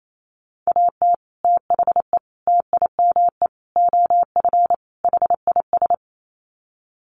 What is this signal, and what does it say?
Morse code, heard as a keyed tone, about 28 wpm, 700 Hz. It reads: AT THE TIME OF HIS
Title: Sets of 5 Words Encore2 28wpm